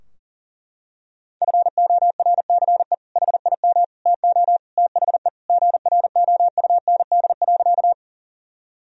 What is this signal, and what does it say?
Morse code, heard as a keyed tone, about 40 wpm, 700 Hz. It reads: FORCE HIM TO THE GROUND.
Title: Sentences from Top 1000 Words - Review Part 3 - 40wpm